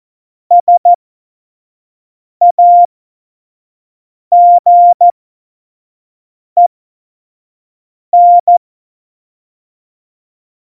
Morse code, heard as a keyed tone, about 14 wpm, spaced as for 5 wpm, 700 Hz. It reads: SAGEN
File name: Morsekurs / Witze / Witz 21. November 2017, 25 F70